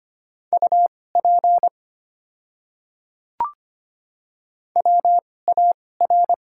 Morse code, heard as a keyed tone, about 25 wpm, spaced as for 18 wpm, 700 Hz. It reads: UP WAR
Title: 2x Character Spacing - Top 500 Words - 25wpm